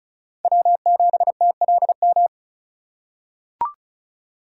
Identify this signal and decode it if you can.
Morse code, heard as a keyed tone, about 34 wpm, 700 Hz. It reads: W7TLM
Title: USA Call Signs - 5 Characters - 4x Character Spacing - 35wpm